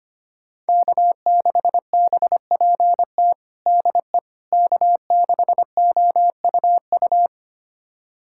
Morse code, heard as a keyed tone, about 25 wpm, 700 Hz. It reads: K6BPT DE K6OUU
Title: USA Callsign DE Callsign - 5 Characters - 25wpm